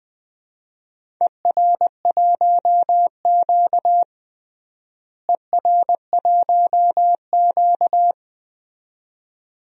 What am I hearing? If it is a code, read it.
Morse code, heard as a keyed tone, about 20 wpm, 700 Hz. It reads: ER1Q ER1Q
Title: DXCC-100 Call Signs - 4 Characters - Repeated 2x - 20wpm